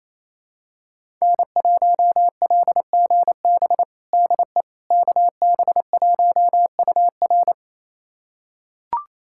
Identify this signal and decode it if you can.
Morse code, heard as a keyed tone, about 28 wpm, 700 Hz. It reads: N1LGB DE KB1UR
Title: USA Callsign DE Callsign - 5 Characters - 28wpm